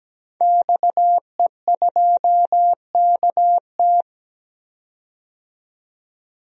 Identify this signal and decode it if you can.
Morse code, heard as a keyed tone, about 17 wpm, 700 Hz. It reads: XE2KT
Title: DXCC-100 Call Signs - 5 Characters - 17wpm